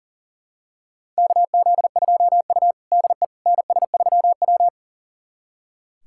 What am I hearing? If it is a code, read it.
Morse code, heard as a keyed tone, about 40 wpm, 700 Hz. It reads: KZ2U DE NS3W
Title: USA Callsign DE Callsign - 4 Characters - 40wpm